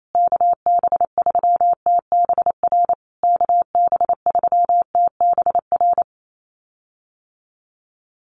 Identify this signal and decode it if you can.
Morse code, heard as a keyed tone, about 28 wpm, 700 Hz. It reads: KB3TBR KB3TBR
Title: USA Call Signs Repeated 3x 28wpm